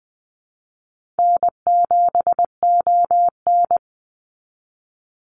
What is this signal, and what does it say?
Morse code, heard as a keyed tone, about 20 wpm, 700 Hz. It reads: N7ON
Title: Call Signs Repeated 3x - 20wpm